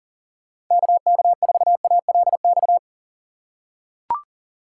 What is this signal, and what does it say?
Morse code, heard as a keyed tone, about 37 wpm, 690 Hz. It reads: KK4ALX
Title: USA Call Signs - 6 Characters - 2x Character Spacing - 40wpm